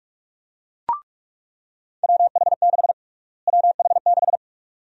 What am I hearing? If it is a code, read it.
Morse code, heard as a keyed tone, about 48 wpm, 700 Hz. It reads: WH6 WH6
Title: USA Call Sign Prefixes - Repeated 2x - 45wpm